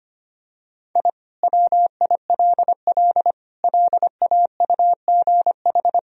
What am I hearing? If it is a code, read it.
Morse code, heard as a keyed tone, about 25 wpm, 700 Hz. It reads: I WILL LAUGH
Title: Sentences from Top 1000 Words - Review Part 1 - 25wpm